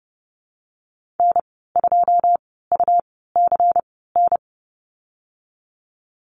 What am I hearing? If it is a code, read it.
Morse code, heard as a keyed tone, about 32 wpm, 700 Hz. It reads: N 2 U C N
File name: Lesson 19g - Letter w - Call Signs 30wpm